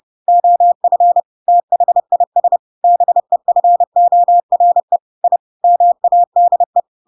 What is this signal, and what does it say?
Morse code, heard as a keyed tone, about 30 wpm, 700 Hz. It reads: OF THIS BEFORE I MADE
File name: Pinocchio / Pinocchio ch3 30 30 ik1hge